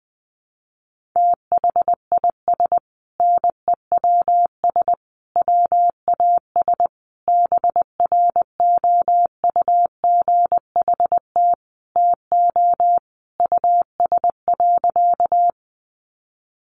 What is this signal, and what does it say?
Morse code, heard as a keyed tone, about 19 wpm, 700 Hz. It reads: THIS NEWS WAS BROUGHT TO US.
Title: Sentences from Top 800 Words 20wpm